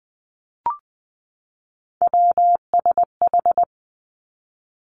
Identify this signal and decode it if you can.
Morse code, heard as a keyed tone, about 20 wpm, 700 Hz. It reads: WSH
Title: Three Challenging Letter-Number Combinations 20wpm